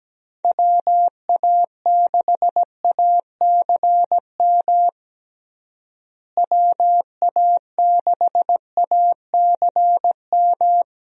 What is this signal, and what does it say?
Morse code, heard as a keyed tone, about 17 wpm, 700 Hz. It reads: WA6ACM WA6ACM
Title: USA Call Signs - 6 Characters - Repeated 3x - 17wpm